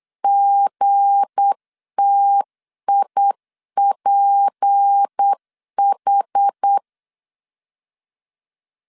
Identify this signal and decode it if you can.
Morse code, heard as a keyed tone, about 8 wpm, 800 Hz. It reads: GTIPH
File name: LEZIONI / 7 GF9Z / 7 G 8wpm 3118